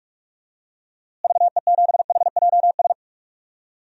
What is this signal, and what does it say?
Morse code, heard as a keyed tone, about 41 wpm, 700 Hz. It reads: VE7HJS
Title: DXCC-100 Call Signs - 6 Characters - Repeated 2x - 45wpm